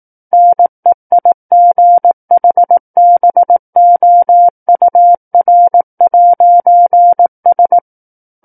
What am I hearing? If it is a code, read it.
Morse code, heard as a keyed tone, about 18 wpm, 700 Hz. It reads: NEIGHBOUR'S
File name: 18 WPM